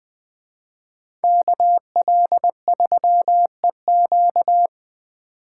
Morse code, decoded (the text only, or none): KL3EQ